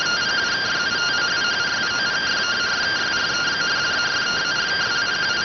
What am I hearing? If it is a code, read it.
Two-tone teleprinter signal, about 170 Hz shift.